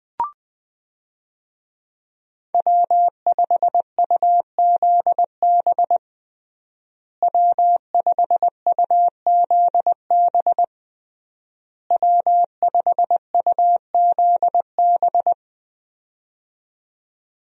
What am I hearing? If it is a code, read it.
Morse code, heard as a keyed tone, about 20 wpm, 700 Hz. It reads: W5UZB W5UZB W5UZB
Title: USA Call Signs - 5 Characters - Repeated 3x - 20wpm